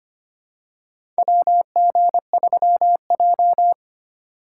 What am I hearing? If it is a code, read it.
Morse code, heard as a keyed tone, about 25 wpm, 700 Hz. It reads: WG3J
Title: USA Call Signs - 4 Characters - 2x Character Spacing - 25wpm